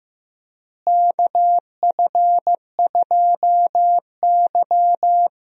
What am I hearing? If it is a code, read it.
Morse code, heard as a keyed tone, about 16 wpm, 700 Hz. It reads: KF2Y
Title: USA Call Signs - 4 Characters - Repeated 2x - 15wpm